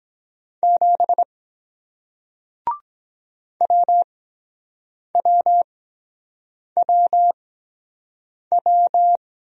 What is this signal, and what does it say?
Morse code, heard as a keyed tone, about 23 wpm, spaced as for 7 wpm, 700 Hz. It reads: 7 WWWW